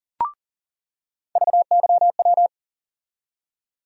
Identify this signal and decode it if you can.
Morse code, heard as a keyed tone, about 40 wpm, 700 Hz. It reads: VYW
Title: USA Call Sign Postfixes - 3 Letters 40wpm